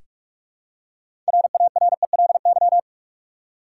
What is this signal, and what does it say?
Morse code, heard as a keyed tone, about 46 wpm, 700 Hz. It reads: RARELY
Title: Six Letter Words - 45wpm